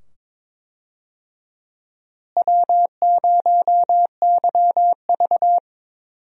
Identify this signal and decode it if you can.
Morse code, heard as a keyed tone, about 21 wpm, 700 Hz. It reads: W0YV
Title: USA Call Signs - 4 Characters - Repeated 2x - 22wpm